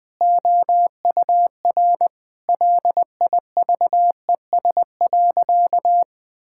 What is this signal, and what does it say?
Morse code, heard as a keyed tone, about 20 wpm, 700 Hz. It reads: OUR LIVES.
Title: Difficult Sentences with Hints - 20wpm